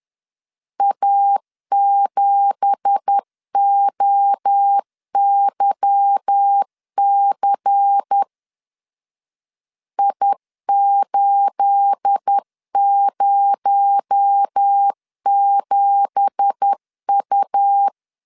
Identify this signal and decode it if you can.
Morse code, heard as a keyed tone, about 10 wpm, 800 Hz. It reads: A7OYC I807U
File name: LEZIONI / 9 2867 10wpm 2733